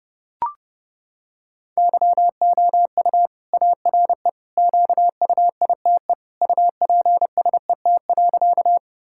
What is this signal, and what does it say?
Morse code, heard as a keyed tone, about 30 wpm, 700 Hz. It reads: YOU ARE QUITE UPSET.
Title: Sentences from Top 1000 Words - Review Part 5 - 30wpm